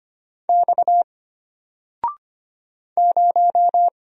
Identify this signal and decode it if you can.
Morse code, heard as a keyed tone, about 25 wpm, 700 Hz. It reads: X 0
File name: Single-Letter-Number-25wpm